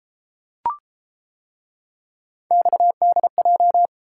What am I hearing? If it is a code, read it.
Morse code, heard as a keyed tone, about 31 wpm, 700 Hz. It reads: XDJ